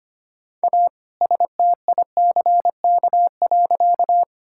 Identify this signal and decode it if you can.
Morse code, heard as a keyed tone, about 25 wpm, 700 Hz. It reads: A STICK.
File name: practice / 25wpm / All-Age Appropriate Jokes - 25wpm